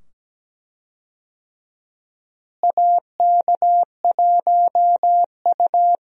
Morse code, heard as a keyed tone, about 17 wpm, 700 Hz. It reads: AK1U
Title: USA Call Signs - 4 Characters - Repeated 2x - 17wpm